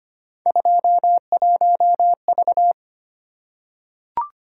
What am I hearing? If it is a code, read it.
Morse code, heard as a keyed tone, about 25 wpm, 700 Hz. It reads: 21V
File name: Three Letter-Number Combinations - 25wpm